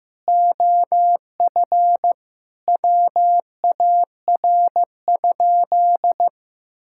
Morse code, heard as a keyed tone, about 15 wpm, 700 Hz. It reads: OF WAR?
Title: Sentences from Top 1000 Words - Review Part 5 - 15wpm